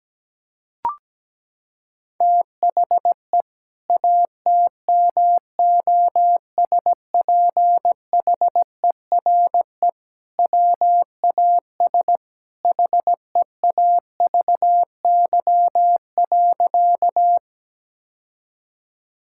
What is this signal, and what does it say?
Morse code, heard as a keyed tone, about 17 wpm, 700 Hz. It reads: THE ATMOSPHERE WAS HEAVY.
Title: Sentences from Top 1600 Words 17wpm